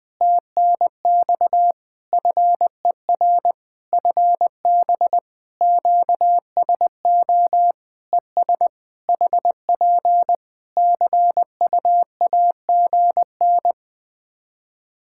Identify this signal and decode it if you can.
Morse code, heard as a keyed tone, about 21 wpm, 700 Hz. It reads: TNX FER FB QSO ES HP CUAGN
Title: Practice QSO - Exchange 4 - Ending - 20wpm